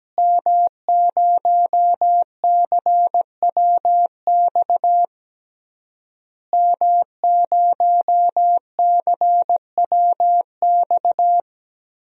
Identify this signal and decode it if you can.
Morse code, heard as a keyed tone, about 17 wpm, 700 Hz. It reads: M0CWX M0CWX
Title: DXCC-100 Call Signs - 5 Characters - Repeated 3x - 17wpm